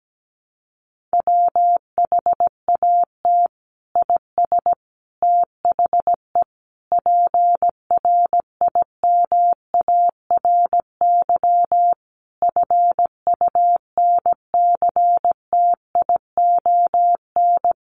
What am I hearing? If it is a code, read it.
Morse code, heard as a keyed tone, about 17 wpm, 700 Hz. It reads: WHAT IS THE PRIMARY FUNCTION
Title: Sentences from Top 1100 Words 17wpm